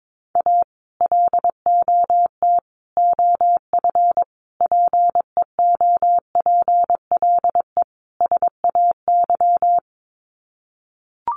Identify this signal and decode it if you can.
Morse code, heard as a keyed tone, about 22 wpm, 700 Hz. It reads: A LOT OF PEOPLE SAY
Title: Sets of 5 Words Encore2 22wpm